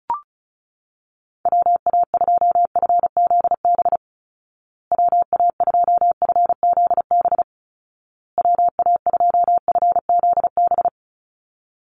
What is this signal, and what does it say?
Morse code, heard as a keyed tone, about 34 wpm, 700 Hz. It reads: WA2FZB WA2FZB WA2FZB
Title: Call Signs Repeated 2x - 35wpm